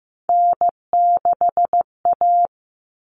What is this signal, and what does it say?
Morse code, heard as a keyed tone, about 15 wpm, 700 Hz. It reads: N6A
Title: All Cut Numbers - 3-Digit 15wpm